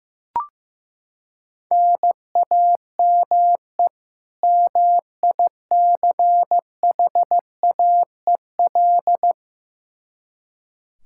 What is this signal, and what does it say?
Morse code, heard as a keyed tone, about 16 wpm, 700 Hz. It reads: NAME MICHAEL
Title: Practice QSO - Exchange 1 - Beginning - 15wpm